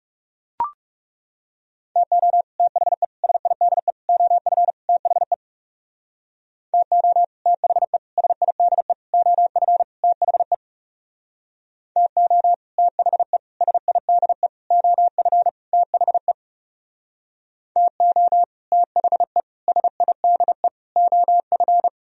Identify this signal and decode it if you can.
Morse code, heard as a keyed tone, about 36 wpm, 700 Hz. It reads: TO THE SIDE OF THE TO THE SIDE OF THE TO THE SIDE OF THE TO THE SIDE OF
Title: Speed-Racing - Sets of 5 Words 30wpm